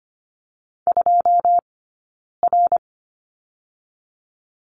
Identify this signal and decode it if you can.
Morse code, heard as a keyed tone, about 25 wpm, spaced as for 9 wpm, 700 Hz. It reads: 2R
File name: course / 25:10wpm / 099 Lesson 17d - Letter u - Two Characters 25wpm